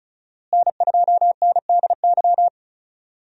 Morse code, heard as a keyed tone, about 36 wpm, 700 Hz. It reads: N2NDY
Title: USA Call Signs - 5 Characters - Repeated 2x - 35wpm